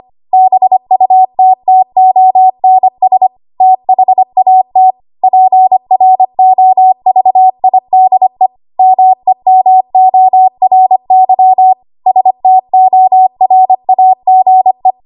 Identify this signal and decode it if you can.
Morse code, heard as a keyed tone, about 25 wpm, 750 Hz. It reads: BUTTONS THAT PROVIDE MEMORY STORAGE
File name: Archive / 25 WPM / 25WPM